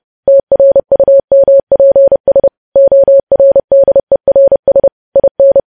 Morse code, heard as a keyed tone, about 30 wpm, 550 Hz. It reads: TRUMPS ORDERS IN
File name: podcast / News headlines-1x-30x30